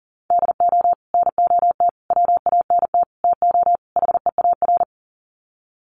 Morse code, heard as a keyed tone, about 40 wpm, 700 Hz. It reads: DO NOT WANT TO HEAR
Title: Sets of 5 Words 40wpm